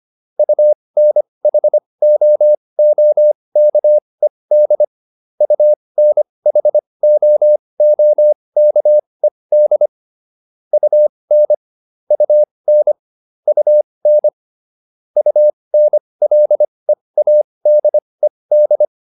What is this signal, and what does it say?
Morse code, heard as a keyed tone, about 25 wpm, spaced as for 20 wpm, 600 Hz. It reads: UNHOOKED UNHOOKED UN UN UN UNLEADED